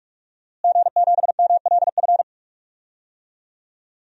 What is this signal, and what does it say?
Morse code, heard as a keyed tone, about 44 wpm, 700 Hz. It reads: G7MLF
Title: DXCC-100 Call Signs - 5 Characters - Repeated 3x - 45wpm